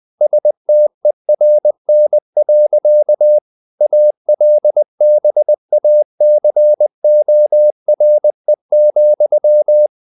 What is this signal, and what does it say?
Morse code, heard as a keyed tone, about 20 wpm, 600 Hz. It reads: STERN. ALBACORE,
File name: the old man and the sea / the old man and the sea part 1 20wpm